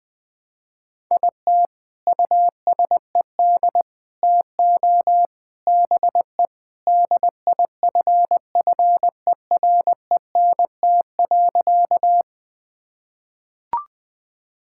Morse code, heard as a keyed tone, about 20 wpm, 700 Hz. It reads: IT USED TO BE DIFFERENT.
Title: Sentences with 4-Word Phrases - 20wpm